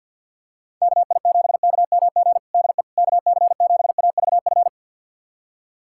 Morse code, heard as a keyed tone, about 50 wpm, 690 Hz. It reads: KI7KMG DE KC7AVF
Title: USA Callsign DE Callsign - 6 Characters - 50wpm